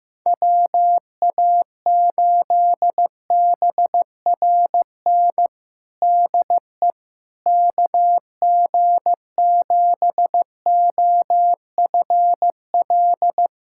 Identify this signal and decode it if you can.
Morse code, heard as a keyed tone, about 16 wpm, 700 Hz. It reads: WA8BRN DE KG7OFL
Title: USA Callsign DE Callsign - 6 Characters - 15wpm